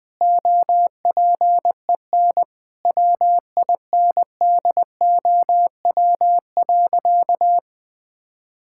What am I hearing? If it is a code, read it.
Morse code, heard as a keyed tone, about 20 wpm, 700 Hz. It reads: OPEN WINDOW.
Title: Sentences from Top 1600 Words 20wpm